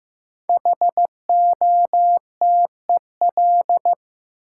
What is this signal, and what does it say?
Morse code, heard as a keyed tone, about 15 wpm, 700 Hz. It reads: HOTEL
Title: Sentences from Top 1600 Words - New Word Drill - 15wpm